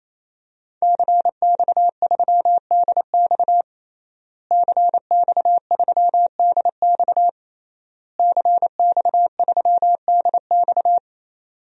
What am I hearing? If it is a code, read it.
Morse code, heard as a keyed tone, about 27 wpm, 700 Hz. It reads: CX3DX CX3DX CX3DX
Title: DXCC-100 Call Signs - 5 Characters - Repeated 3x - 28wpm